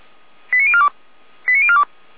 Five-tone selective call (ZVEI), heard as quick runs of five tones, tones of about 70 ms. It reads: 89042 89042